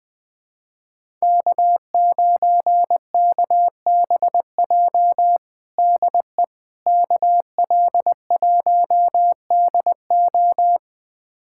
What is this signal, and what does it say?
Morse code, heard as a keyed tone, about 21 wpm, 700 Hz. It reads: K9KBJ DE KL1DO
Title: USA Callsign DE Callsign - 5 Characters - 20wpm